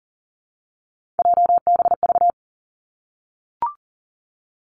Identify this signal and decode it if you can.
Morse code, heard as a keyed tone, about 38 wpm, 700 Hz. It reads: JBV
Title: Three Challenging Letter-Number Combinations 40wpm